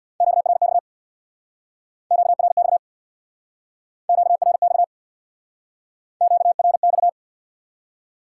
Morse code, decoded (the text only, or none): QRX QRX QRX QRX